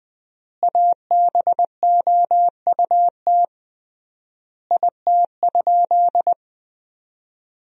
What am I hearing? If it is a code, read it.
Morse code, heard as a keyed tone, about 20 wpm, 700 Hz. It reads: ABOUT IT?
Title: 3x Word-Spacing - Sentences from Top 200 Words - 20wpm